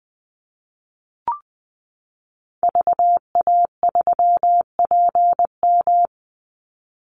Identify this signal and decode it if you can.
Morse code, heard as a keyed tone, about 20 wpm, 700 Hz. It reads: VA3PM
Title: Call Signs - 20wpm